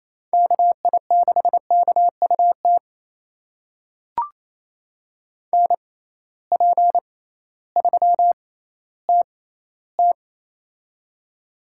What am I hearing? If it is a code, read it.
Morse code, heard as a keyed tone, about 28 wpm, spaced as for 15 wpm, 700 Hz. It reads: KI6KUT NP3TT
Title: USA Call Signs - 6x Character Spacing - 28wpm